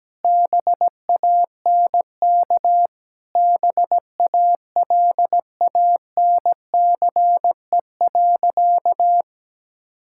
Morse code, heard as a keyed tone, about 17 wpm, 700 Hz. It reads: BANK BALANCE.
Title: Sentences from Top 1000 Words - Review Part 6 - 17wpm